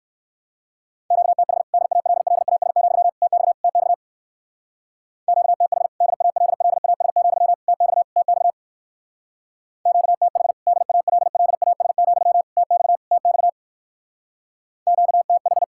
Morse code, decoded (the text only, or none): QTH DALLAS, TX TX QTH DALLAS, TX TX QTH DALLAS, TX TX QTH